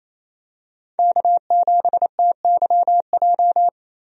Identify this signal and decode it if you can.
Morse code, heard as a keyed tone, about 28 wpm, 700 Hz. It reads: K7TYJ
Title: USA Call Signs - 5 Characters - Repeated 2x - 28wpm